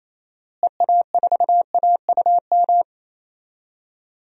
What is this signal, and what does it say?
Morse code, heard as a keyed tone, about 28 wpm, 700 Hz. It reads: EA4AUM
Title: DXCC-100 Call Signs - 6 Characters - Repeated 3x - 28wpm